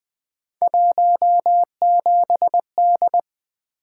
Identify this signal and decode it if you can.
Morse code, heard as a keyed tone, about 20 wpm, 700 Hz. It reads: 17D
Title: Three Letter-Number Combinations - 20wpm